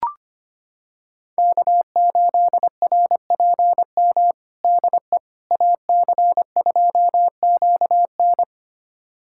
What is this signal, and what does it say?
Morse code, heard as a keyed tone, about 25 wpm, 700 Hz. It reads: K8RPM DE AC2QN
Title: USA Callsign DE Callsign - 5 Characters - 25wpm